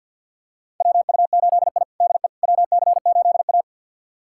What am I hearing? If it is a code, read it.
Morse code, heard as a keyed tone, about 50 wpm, 700 Hz. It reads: WU8I DE WC8A
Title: USA Callsign DE Callsign - 4 Characters - 50wpm